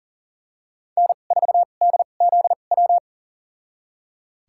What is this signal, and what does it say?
Morse code, heard as a keyed tone, about 40 wpm, spaced as for 29 wpm, 700 Hz. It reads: N4DZW
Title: USA Call Signs - 5 Characters - 2x Character Spacing - 40wpm